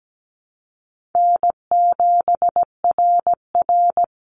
Morse code, heard as a keyed tone, about 17 wpm, 700 Hz. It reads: N7RR
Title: Easy Call Signs - 17wpm